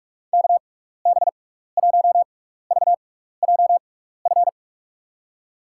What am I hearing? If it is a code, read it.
Morse code, heard as a keyed tone, about 46 wpm, spaced as for 15 wpm, 700 Hz. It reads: KB1VJF
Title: USA Call Signs - 6 Characters - 6x Character Spacing - 45wpm